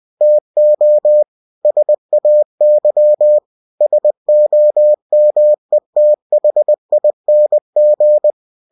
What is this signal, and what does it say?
Morse code, heard as a keyed tone, about 20 wpm, 600 Hz. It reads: TO SAY SOMETHING